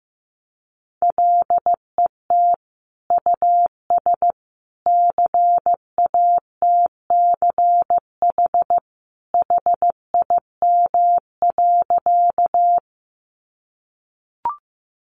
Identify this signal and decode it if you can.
Morse code, heard as a keyed tone, about 15 wpm, 700 Hz. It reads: LET US CATCH HIM.
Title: Sentences from Top 1500 Words 15wpm